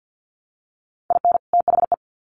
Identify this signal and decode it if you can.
Morse code, heard as a keyed tone, about 50 wpm, 700 Hz. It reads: IN THE
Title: Sets of 2 Words 50wpm